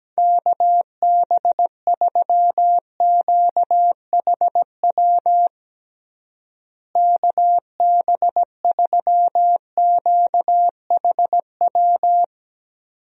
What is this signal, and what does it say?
Morse code, heard as a keyed tone, about 17 wpm, 700 Hz. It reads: KB3QHW KB3QHW
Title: USA Call Signs - 6 Characters - Repeated 2x - 17wpm